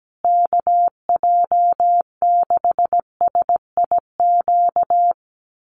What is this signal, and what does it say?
Morse code, heard as a keyed tone, about 17 wpm, 700 Hz. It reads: KJ6SIQ
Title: USA Call Signs Repeated 2x 17wpm